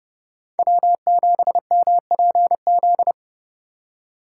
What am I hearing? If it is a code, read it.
Morse code, heard as a keyed tone, about 32 wpm, 700 Hz. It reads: W7MPZ
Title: USA Call Signs - 5 Characters - 4x Character Spacing - 30wpm